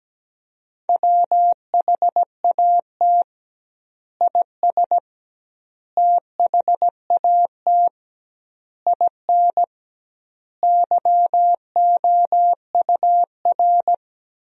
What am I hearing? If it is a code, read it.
Morse code, heard as a keyed tone, about 17 wpm, 700 Hz. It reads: WHAT IS THAT IN YOUR
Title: 2x Word-Spacing - Sentences from Top 200 Words - 17wpm